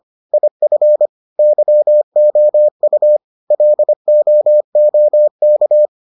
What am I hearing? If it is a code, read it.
Morse code, heard as a keyed tone, about 25 wpm, 600 Hz. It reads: IF YOU LOOK